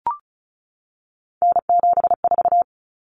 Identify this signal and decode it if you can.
Morse code, heard as a keyed tone, about 34 wpm, 700 Hz. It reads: N74
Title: Common Cut Numbers - 3-Digit 35wpm